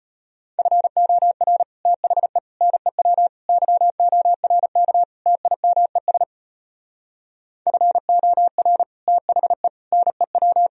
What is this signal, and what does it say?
Morse code, heard as a keyed tone, about 33 wpm, 700 Hz. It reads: FOR THE NEW YORK TIMES FOR THE NEW